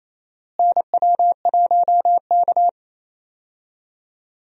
Morse code, heard as a keyed tone, about 28 wpm, 700 Hz. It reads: NW1K
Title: USA Call Signs - 4 Characters - Repeated 3x - 28wpm